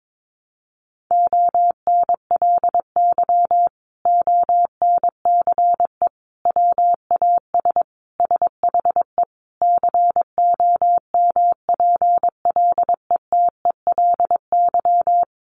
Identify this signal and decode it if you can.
Morse code, heard as a keyed tone, about 22 wpm, 700 Hz. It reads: ONLY ONCE WAS SHE COMPLETELY
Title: Sentences from Top 1100 Words 22wpm